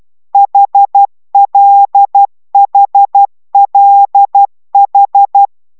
Morse code, heard as a keyed tone, about 12 wpm, 800 Hz. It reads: HLHLH
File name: taviro tanulas / cw 69 10 10